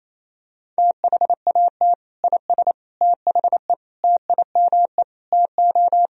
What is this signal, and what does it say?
Morse code, heard as a keyed tone, about 28 wpm, 700 Hz. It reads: THAT IS THE TIME TO
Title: Sentences from Top 1000 Words - Review Part 1 - 28wpm